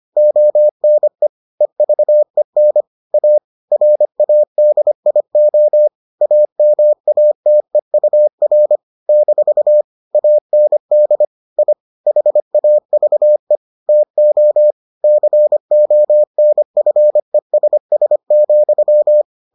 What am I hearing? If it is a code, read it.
Morse code, heard as a keyed tone, about 25 wpm, 600 Hz. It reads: ONE EVEN A RADIO AMATEUR - AND I HAVE TO CONFESS,